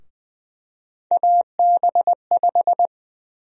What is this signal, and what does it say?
Morse code, heard as a keyed tone, about 20 wpm, 700 Hz. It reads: AB5
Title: USA Call Sign Prefixes - 20wpm